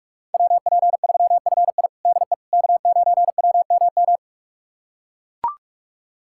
Morse code, heard as a keyed tone, about 43 wpm, 700 Hz. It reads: WP3FI DE K9WMM
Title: USA Callsign DE Callsign - 5 Characters - 45wpm